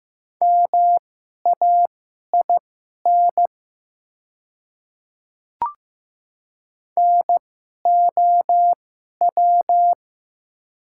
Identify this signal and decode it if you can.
Morse code, heard as a keyed tone, about 15 wpm, spaced as for 11 wpm, 700 Hz. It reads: MAIN NOW
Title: 2x Character Spacing - Top 300 Words - 15wpm